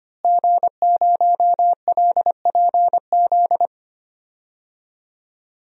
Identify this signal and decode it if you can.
Morse code, heard as a keyed tone, about 25 wpm, 700 Hz. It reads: G0LPZ
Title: DXCC-100 Call Signs - 5 Characters - Repeated 3x - 25wpm